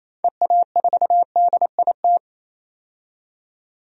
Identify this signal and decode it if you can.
Morse code, heard as a keyed tone, about 27 wpm, 700 Hz. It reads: EA4DIT
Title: DXCC-100 Call Signs - 6 Characters - 28wpm